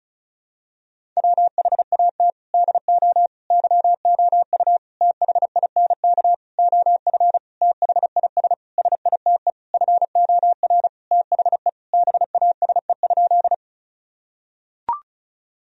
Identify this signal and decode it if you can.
Morse code, heard as a keyed tone, about 36 wpm, 690 Hz. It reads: WHAT DO YOU THINK OF THIS SITE FOR THE BASE?
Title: Sentences from Top 1000 Words - Review Part 1 - 35wpm